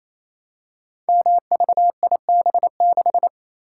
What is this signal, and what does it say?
Morse code, heard as a keyed tone, about 27 wpm, 700 Hz. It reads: MVIB6
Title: Code Groups - 5 Letter:Numbers - 28wpm